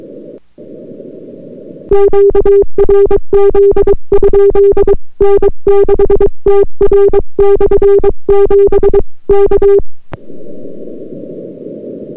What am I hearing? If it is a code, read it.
Morse code, heard as a keyed tone, about 22 wpm, 400 Hz. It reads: QRZ? N6TR/7 K